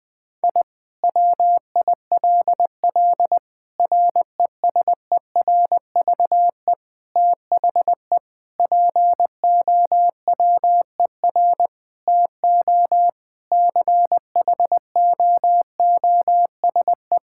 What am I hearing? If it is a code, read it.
Morse code, heard as a keyed tone, about 20 wpm, 700 Hz. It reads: I WILL RESERVE THE POWER TO CHOOSE
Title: Sentences from Top 1000 Words - Review Part 6 - 20wpm